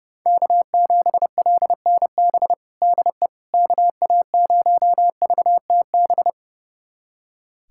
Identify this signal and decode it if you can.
Morse code, heard as a keyed tone, about 30 wpm, 700 Hz. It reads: K7LNB DE KA0VTB
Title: USA Callsign DE Callsign 30wpm